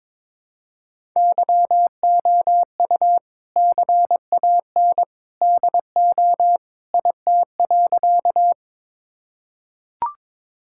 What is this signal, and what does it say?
Morse code, heard as a keyed tone, about 21 wpm, 700 Hz. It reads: YOU CAN DO IT.
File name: practice / 22wpm / Sentences with 2-Word Phrases - 22wpm